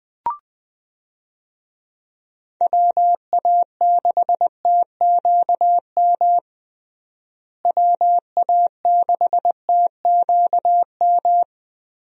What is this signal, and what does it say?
Morse code, heard as a keyed tone, about 21 wpm, 700 Hz. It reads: WA6TQM WA6TQM
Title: USA Call Signs - 6 Characters - Repeated 2x - 20wpm